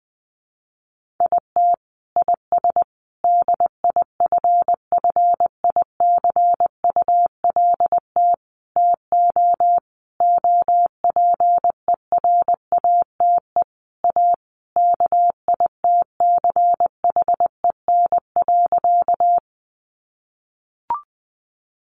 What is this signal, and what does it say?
Morse code, heard as a keyed tone, about 20 wpm, 700 Hz. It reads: IT IS DIFFICULT TO OPERATE A KITCHEN.
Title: Sentences from Top 1500 Words 20wpm